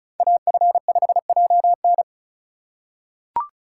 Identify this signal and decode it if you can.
Morse code, heard as a keyed tone, about 36 wpm, 700 Hz. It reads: AF5JN
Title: USA Call Signs - 4x Character Spacing - 35wpm